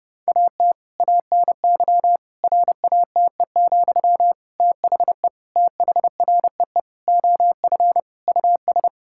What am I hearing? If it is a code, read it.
Morse code, heard as a keyed tone, about 30 wpm, 700 Hz. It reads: AT ANY RATE, THE THREE OF US
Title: Sentences from Top 1000 Words - Review Part 6 - 30wpm